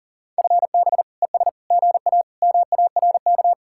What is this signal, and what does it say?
Morse code, heard as a keyed tone, about 50 wpm, 700 Hz. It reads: FB ES GA MARK
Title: Speed-Racing Common QSO Phrases Encore - 40wpm